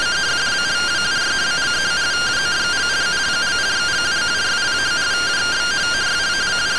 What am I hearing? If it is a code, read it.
Two-tone teleprinter signal, about 170 Hz shift.
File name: findings / MR CORONAL 55.gen-21 rtty rx